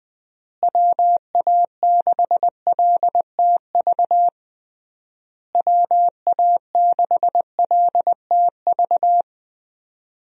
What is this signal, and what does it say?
Morse code, heard as a keyed tone, about 20 wpm, 700 Hz. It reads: WA6LTV WA6LTV
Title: USA Call Signs - 6 Characters - Repeated 3x - 20wpm